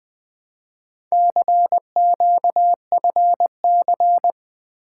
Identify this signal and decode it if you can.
Morse code, heard as a keyed tone, about 20 wpm, 700 Hz. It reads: CQFC
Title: Four Letter-Number Combinations - 20wpm